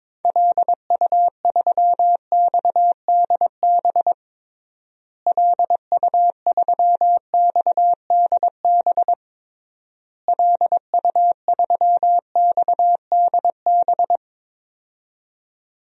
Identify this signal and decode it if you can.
Morse code, heard as a keyed tone, about 22 wpm, 700 Hz. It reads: LU3XDB LU3XDB LU3XDB
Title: DXCC-100 Call Signs - 6 Characters - Repeated 3x - 22wpm